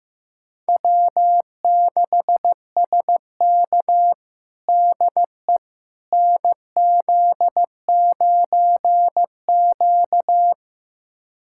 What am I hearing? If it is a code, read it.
Morse code, heard as a keyed tone, about 15 wpm, 700 Hz. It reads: W6SK DE NZ9Q
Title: USA Callsign DE Callsign - 4 Characters - 15wpm